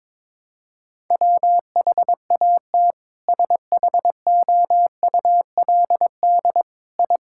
Morse code, heard as a keyed tone, about 22 wpm, 700 Hz. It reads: WHAT SHOULD I
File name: practice / 22wpm / Sentences with 4-Word Phrases - 22wpm